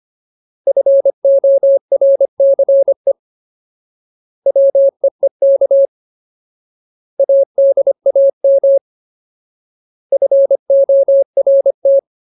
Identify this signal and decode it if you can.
Morse code, heard as a keyed tone, about 25 wpm, 550 Hz. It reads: FORCE WEEK ADAM FORT
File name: Ultimate Word List / 25WPM Ultimate